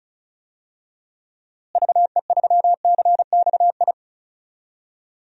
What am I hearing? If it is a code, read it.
Morse code, heard as a keyed tone, about 34 wpm, 700 Hz. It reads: VE3CXI
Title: DXCC-100 Call Signs - 6 Characters - Repeated 3x - 35wpm